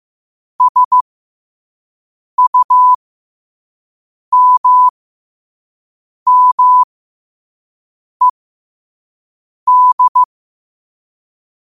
Morse code, heard as a keyed tone, about 15 wpm, spaced as for 5 wpm, 1000 Hz. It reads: SUMMED
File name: qotdfiles / 5wpm